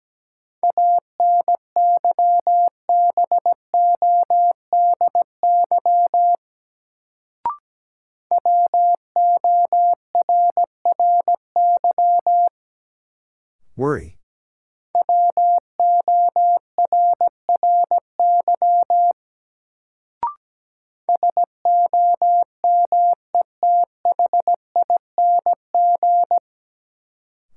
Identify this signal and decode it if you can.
Morse code, heard as a keyed tone, about 17 wpm, 700 Hz. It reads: ANYBODY WORRY WORRY SOMETHING
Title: Sets of 5 Words - New Word Drill - 17wpm